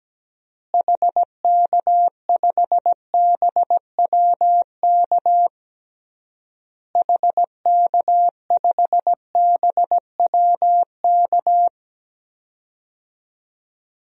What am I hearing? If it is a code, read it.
Morse code, heard as a keyed tone, about 17 wpm, 700 Hz. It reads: HK5BWK HK5BWK
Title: DXCC-100 Call Signs - 6 Characters - Repeated 3x - 17wpm